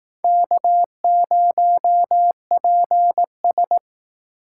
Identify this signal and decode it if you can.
Morse code, heard as a keyed tone, about 18 wpm, 700 Hz. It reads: K0PS
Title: Speed Racing - USA 1x2 Call Signs 15wpm